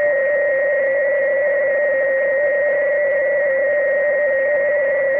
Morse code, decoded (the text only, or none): none